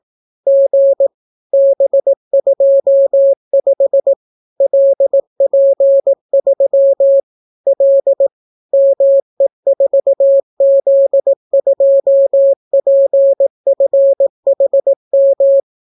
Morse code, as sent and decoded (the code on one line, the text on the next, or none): --. / -... ..--- ..... / .-.. .--. ...-- / .-.. / -- . ....- --.. ..--- .--. ..-. .... --
G B25 LP3 L ME4Z2PFHM